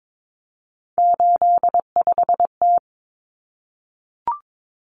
Morse code, heard as a keyed tone, about 22 wpm, 700 Hz. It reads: 85T
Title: Common Cut Numbers - 3-Digit 22wpm